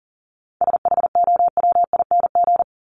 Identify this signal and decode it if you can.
Morse code, heard as a keyed tone, about 40 wpm, 700 Hz. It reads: SHOWING
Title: ING Suffix - Top 100 Words with Base Words 40wpm